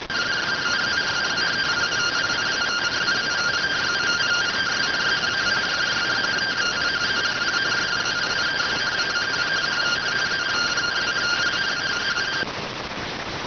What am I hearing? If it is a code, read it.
Two-tone teleprinter signal, about 170 Hz shift.